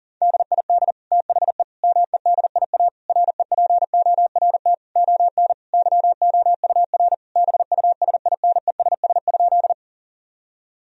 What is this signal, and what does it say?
Morse code, heard as a keyed tone, about 40 wpm, 700 Hz. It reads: DID THE MEDIA REPORT ON YOUR BUSINESS?
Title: Sentences from Top 1000 Words - Review Part 4 - 40wpm